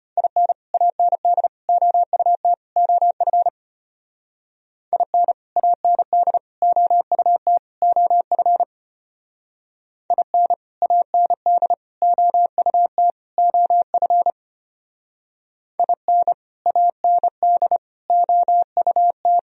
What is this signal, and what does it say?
Morse code, heard as a keyed tone, about 32 wpm, 700 Hz. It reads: IN AND OUT OF IN AND OUT OF IN AND OUT OF IN AND OUT